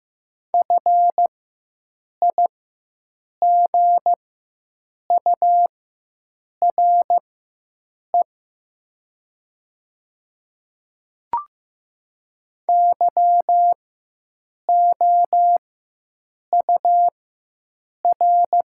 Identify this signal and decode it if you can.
Morse code, heard as a keyed tone, about 15 wpm, spaced as for 7 wpm, 700 Hz. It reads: FIGURE YOUR
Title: 4x Character Spacing - Top 500 Words - 15wpm